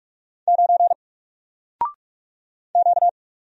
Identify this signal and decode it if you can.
Morse code, heard as a keyed tone, about 44 wpm, 700 Hz. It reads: 9 Q
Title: Single Character (Letter:Number:Punctuation:Prosign) 45wpm